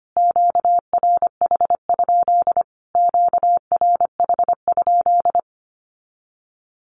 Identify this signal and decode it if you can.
Morse code, heard as a keyed tone, about 25 wpm, 700 Hz. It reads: QRH? QRH?
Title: Q-Codes Repeated 3x 25wpm